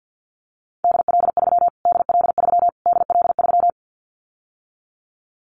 Morse code, heard as a keyed tone, about 50 wpm, 700 Hz. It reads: DL3 DL3 DL3
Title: Callsign Prefixes for Top 25 DXCC - Repeated 3x - 50wpm